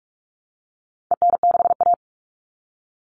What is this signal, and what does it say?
Morse code, heard as a keyed tone, about 44 wpm, 700 Hz. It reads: EN6A
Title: All Cut Numbers - 4-Digit 45wpm